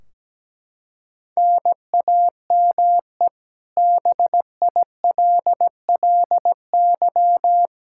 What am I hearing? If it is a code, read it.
Morse code, heard as a keyed tone, about 17 wpm, 700 Hz. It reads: NAME BILLY
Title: Practice QSO - Exchange 1 - Beginning - 17wpm